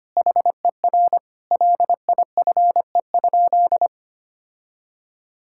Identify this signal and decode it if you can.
Morse code, heard as a keyed tone, about 25 wpm, 700 Hz. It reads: HER LIFE?
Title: Sentences from Top 1000 Words - Review Part 6 - 25wpm